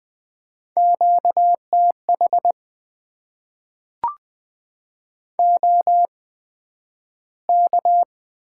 Morse code, heard as a keyed tone, about 20 wpm, spaced as for 11 wpm, 700 Hz. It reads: QTH OK
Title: Top 90 QSO Elements - 8x Character Spacing - 20wpm